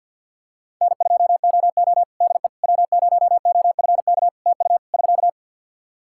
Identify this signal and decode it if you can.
Morse code, heard as a keyed tone, about 50 wpm, 700 Hz. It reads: N2OO DE W0OFK TU <SK>
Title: K1USN SST QSO Practice 50wpm